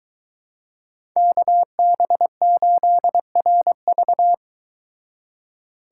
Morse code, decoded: KB8RV

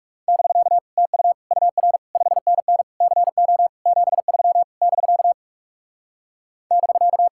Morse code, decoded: <BK> TU UR 5NN CO 73 <BK> <BK>

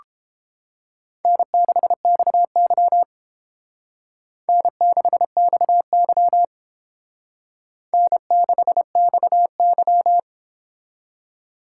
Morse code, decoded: N6XY N6XY N6XY

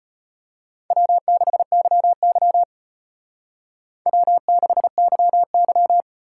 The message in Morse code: W6YY W6YY

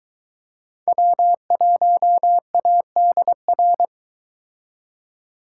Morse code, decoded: W1ADR

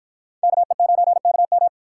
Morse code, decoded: KE9KM

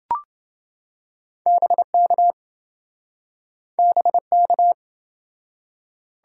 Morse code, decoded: BK BK

